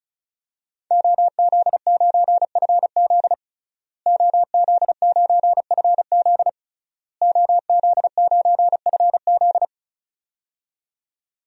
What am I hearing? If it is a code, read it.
Morse code, heard as a keyed tone, about 36 wpm, 700 Hz. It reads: OZ9FZ OZ9FZ OZ9FZ
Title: DXCC-100 Call Signs - 5 Characters - Repeated 3x - 35wpm